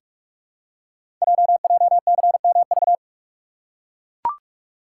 Morse code, heard as a keyed tone, about 44 wpm, 700 Hz. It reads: JJCMV
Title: Code Groups - 5 Letters - 45wpm